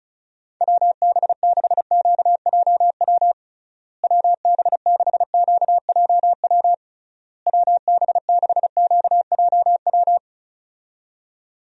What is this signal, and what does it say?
Morse code, heard as a keyed tone, about 34 wpm, 700 Hz. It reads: WB6QJW WB6QJW WB6QJW
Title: USA Call Signs - 6 Characters - Repeated 3x - 35wpm